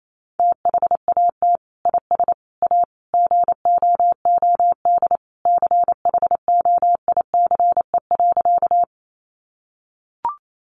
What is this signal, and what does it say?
Morse code, heard as a keyed tone, about 28 wpm, 700 Hz. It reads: THAT IS A GOOD CHOICE.
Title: Sentences from Top 900 Words 28wpm